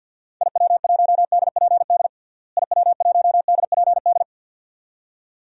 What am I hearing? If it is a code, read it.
Morse code, heard as a keyed tone, about 50 wpm, 700 Hz. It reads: IW1DPD IW1DPD
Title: DXCC-100 Call Signs - 6 Characters - Repeated 2x - 50wpm